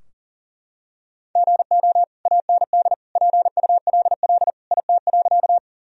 Morse code, heard as a keyed tone, about 37 wpm, 700 Hz. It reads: GO AND PULL IT.
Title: Sentences from Top 1000 Words - Review Part 2 - 40wpm